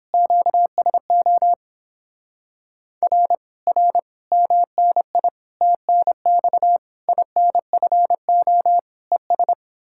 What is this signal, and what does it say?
Morse code, heard as a keyed tone, about 29 wpm, 700 Hz. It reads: QSO R R MNI TNX INFO ES